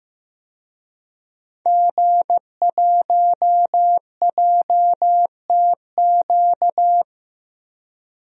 Morse code, heard as a keyed tone, about 15 wpm, 700 Hz. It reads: G1JTQ
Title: DXCC-100 Call Signs - 5 Characters - Repeated 3x - 15wpm